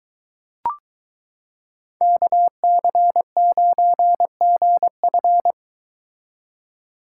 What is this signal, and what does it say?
Morse code, heard as a keyed tone, about 22 wpm, 700 Hz. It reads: KC9GF